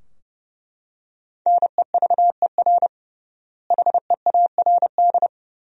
Morse code, heard as a keyed tone, about 32 wpm, 700 Hz. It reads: NEVER HEARD
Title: 3x Word-Spacing - Sentences from Top 400 Words - 30wpm